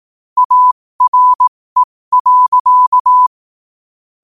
Morse code, heard as a keyed tone, about 18 wpm, spaced as for 15 wpm, 1000 Hz. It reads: ARE.
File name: qotdfiles / 15wpm